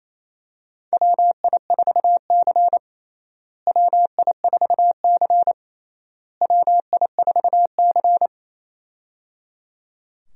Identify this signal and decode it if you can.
Morse code, heard as a keyed tone, about 28 wpm, 700 Hz. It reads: WI4C WI4C WI4C
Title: USA Call Signs - 4 Characters - Repeated 3x - 28wpm